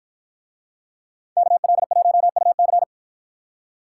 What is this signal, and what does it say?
Morse code, heard as a keyed tone, about 53 wpm, 700 Hz. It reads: KL1UC